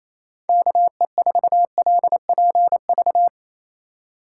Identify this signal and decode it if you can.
Morse code, heard as a keyed tone, about 28 wpm, 700 Hz. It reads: KE4LPV
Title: USA Call Signs - 6 Characters - 3x Character Spacing - 28wpm